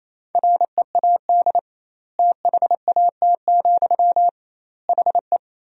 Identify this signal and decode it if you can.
Morse code, heard as a keyed tone, about 27 wpm, 700 Hz. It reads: READ THAT, HE
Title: 2x Word-Spacing - Sentences from Top 200 Words - 28wpm